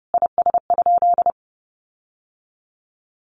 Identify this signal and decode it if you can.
Morse code, heard as a keyed tone, about 30 wpm, 700 Hz. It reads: IS?
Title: Sentences from Top 1500 Words 30wpm